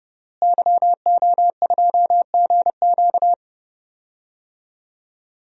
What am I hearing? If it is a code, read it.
Morse code, heard as a keyed tone, about 30 wpm, 700 Hz. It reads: YO2GQ
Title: DXCC-100 Call Signs - 5 Characters - Repeated 2x - 30wpm